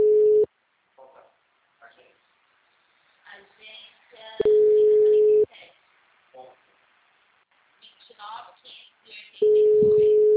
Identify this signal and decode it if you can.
Ringback tone (Europe, 425 Hz), with speech.
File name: out- -2009- - - .70924